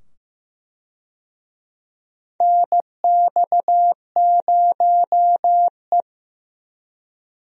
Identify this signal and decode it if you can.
Morse code, heard as a keyed tone, about 15 wpm, 700 Hz. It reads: NX0E
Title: USA Call Signs - 4 Characters - 15wpm